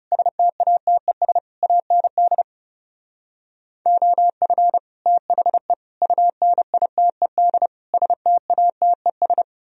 Morse code, heard as a keyed tone, about 38 wpm, 700 Hz. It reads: STATES AND OF THE UNITED STATES
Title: Speed-Racing - Sets of 5 Words 30wpm